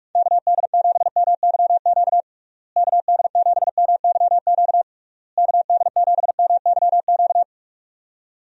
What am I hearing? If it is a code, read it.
Morse code, heard as a keyed tone, about 43 wpm, 700 Hz. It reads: KD7MYQ KD7MYQ KD7MYQ
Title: USA Call Signs - 6 Characters - Repeated 3x - 45wpm